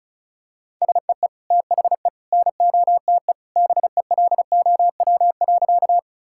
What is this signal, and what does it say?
Morse code, heard as a keyed tone, about 36 wpm, 700 Hz. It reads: SEE THE NOTE BELOW.
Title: Sentences from Top 1000 Words - Review Part 5 - 35wpm